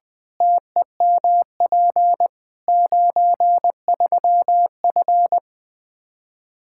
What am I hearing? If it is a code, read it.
Morse code, heard as a keyed tone, about 20 wpm, 700 Hz. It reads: TEMP 93F
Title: K1USN SST QSO Practice 20wpm